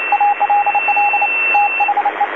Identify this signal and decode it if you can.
Morse code, heard as a keyed tone, about 29 wpm, 820 Hz, with radio noise. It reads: ALL THE
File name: DC-TRCVR / 80-CW